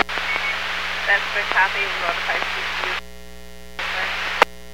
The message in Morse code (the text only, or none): none